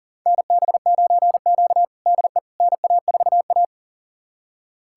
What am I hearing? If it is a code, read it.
Morse code, heard as a keyed tone, about 37 wpm, 700 Hz. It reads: NB9Q DE NA4A
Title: USA Callsign DE Callsign - 4 Characters - 40wpm